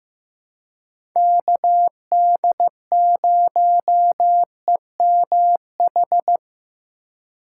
Morse code, heard as a keyed tone, about 15 wpm, 700 Hz. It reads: KD0EMH
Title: USA Call Signs - 6 Characters - 3x Character Spacing - 15wpm